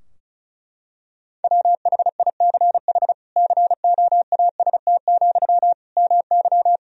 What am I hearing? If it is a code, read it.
Morse code, heard as a keyed tone, about 34 wpm, 700 Hz. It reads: WHICH COAST, MY
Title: Sentences from Top 1000 Words - Review Part 4 - 35wpm